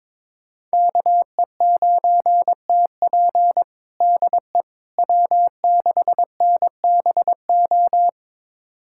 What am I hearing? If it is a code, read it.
Morse code, heard as a keyed tone, about 22 wpm, 700 Hz. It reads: KE9TP DE W6NBO
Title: USA Callsign DE Callsign - 5 Characters - 22wpm